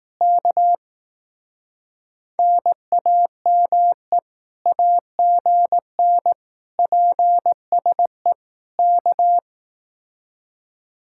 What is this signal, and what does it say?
Morse code, heard as a keyed tone, about 18 wpm, 700 Hz. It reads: K NAME AGN PSE K
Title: Speed-Racing Common QSO Phrases Encore - 15wpm